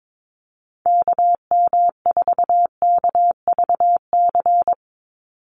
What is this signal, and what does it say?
Morse code, heard as a keyed tone, about 22 wpm, 700 Hz. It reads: KM4KVC
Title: USA Call Signs 22wpm